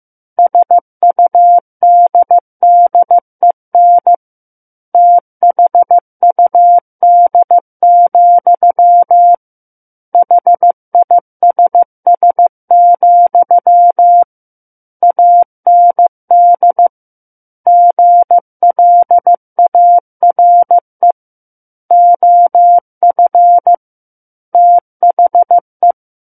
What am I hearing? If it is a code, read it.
Morse code, heard as a keyed tone, about 15 wpm, 700 Hz. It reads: SUDDEN THUD, HISS, AND GLARE OF THE
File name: Book One / 06-Chapter 6 - 15 WPM